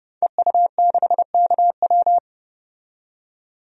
Morse code, heard as a keyed tone, about 30 wpm, 700 Hz. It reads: EU6KW
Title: DXCC-100 Call Signs - 5 Characters - 30wpm